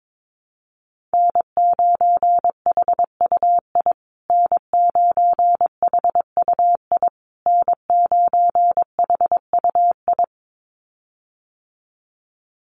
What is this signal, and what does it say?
Morse code, heard as a keyed tone, about 22 wpm, 700 Hz. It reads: N9HUI N9HUI N9HUI
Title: USA Call Signs Repeated 3x 22wpm